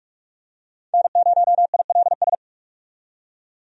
Morse code, heard as a keyed tone, about 41 wpm, 700 Hz. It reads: N0ILS
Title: USA Call Signs - 5 Characters - 45wpm